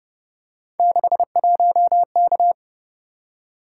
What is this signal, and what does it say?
Morse code, heard as a keyed tone, about 29 wpm, 700 Hz. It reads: 61K